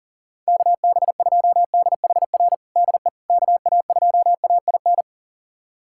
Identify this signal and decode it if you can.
Morse code, heard as a keyed tone, about 40 wpm, 700 Hz. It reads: KB2DHR DE KA2AIN
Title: USA Callsign DE Callsign - 6 Characters - 40wpm